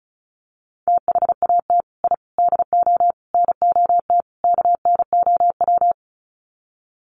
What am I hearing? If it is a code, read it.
Morse code, heard as a keyed tone, about 34 wpm, 700 Hz. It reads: THAT I DO NOT KNOW
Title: Sets of 5 Words Encore2 35wpm